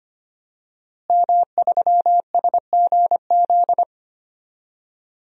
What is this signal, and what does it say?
Morse code, heard as a keyed tone, about 25 wpm, 700 Hz. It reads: M3SGZ
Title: DXCC-100 Call Signs - 5 Characters - Repeated 2x - 25wpm